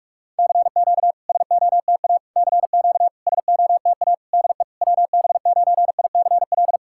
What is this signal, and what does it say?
Morse code, heard as a keyed tone, about 48 wpm, 700 Hz. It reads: CQ SOTA CQ SOTA DE WB9ICL
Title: SOTA QSO Practice 45wpm